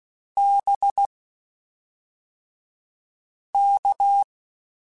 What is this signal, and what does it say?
Morse code, heard as a keyed tone, about 16 wpm, spaced as for 3 wpm, 800 Hz. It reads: BK
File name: QSOs / QSOs-mp3s / WD0BGP 1@5